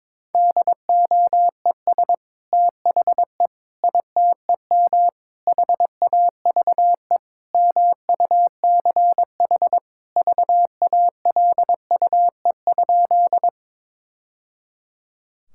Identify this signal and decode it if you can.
Morse code, heard as a keyed tone, about 22 wpm, 700 Hz. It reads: DOES THE ITEM HAVE MUCH VALUE?
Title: Sentences from Top 1000 Words - Review Part 6 - 22wpm